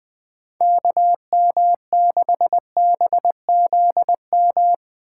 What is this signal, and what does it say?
Morse code, heard as a keyed tone, about 20 wpm, 700 Hz. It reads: KM6BZM
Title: USA Call Signs - 6 Characters - 4x Character Spacing - 20wpm